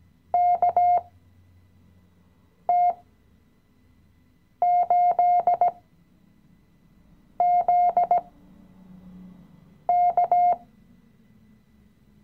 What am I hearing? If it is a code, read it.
Morse code, heard as a keyed tone, about 16 wpm, spaced as for 5 wpm, 710 Hz. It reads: KT8ZK
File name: CW-Online-Kurs / Anfänger / Lektion 018